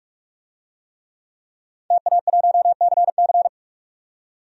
Morse code, heard as a keyed tone, about 46 wpm, 700 Hz. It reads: TA1CC
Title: DXCC-100 Call Signs - 5 Characters - 45wpm